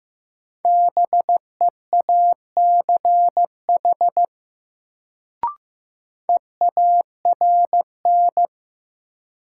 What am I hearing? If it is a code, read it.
Morse code, heard as a keyed tone, about 15 wpm, 700 Hz. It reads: BEACH EARN
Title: Sentences from Top 1500 Words - New Word Drill Alt - 15wpm